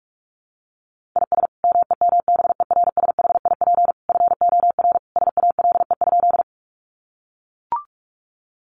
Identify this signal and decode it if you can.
Morse code, heard as a keyed tone, about 44 wpm, 700 Hz. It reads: IS MEMBERSHIP FOR SALE?
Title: Sentences from Top 1400 Words 45wpm